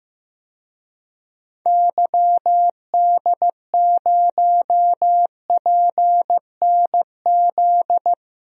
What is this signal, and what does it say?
Morse code, heard as a keyed tone, about 15 wpm, 690 Hz. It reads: YD0PNZ
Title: DXCC-100 Call Signs - 6 Characters - 15wpm